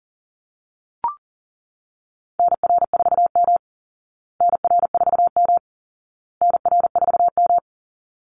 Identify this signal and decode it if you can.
Morse code, heard as a keyed tone, about 40 wpm, 700 Hz. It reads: NR4M NR4M NR4M
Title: Call Signs Repeated 2x - 40wpm